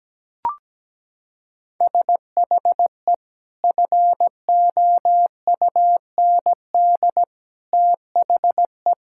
Morse code, heard as a keyed tone, about 17 wpm, 700 Hz. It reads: SHE FOUND THE
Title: Sentences from Top 1000 Words - Review Part 6 - 17wpm